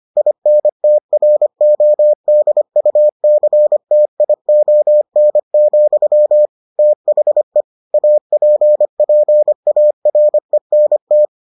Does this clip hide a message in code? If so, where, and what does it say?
Yes: Morse code from 0.2 to 11.3 s, heard as a keyed tone, about 25 wpm, 600 Hz. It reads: INTRODUCTION, THE APPARENT